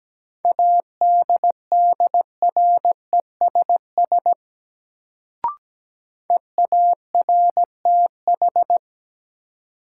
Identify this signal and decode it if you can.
Morse code, heard as a keyed tone, about 17 wpm, 700 Hz. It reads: ADDRESS EARTH
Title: Sets of 5 Words Encore - New Word Drill - 17wpm